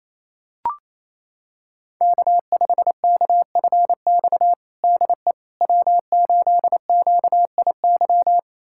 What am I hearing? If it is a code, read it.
Morse code, heard as a keyed tone, about 27 wpm, 700 Hz. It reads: K5KFX DE W8QIY
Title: USA Callsign DE Callsign - 5 Characters - 28wpm